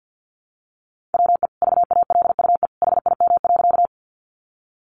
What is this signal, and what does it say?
Morse code, heard as a keyed tone, about 50 wpm, 700 Hz. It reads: WE VALUE HIM.
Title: Sentences from Top 600 Words 50wpm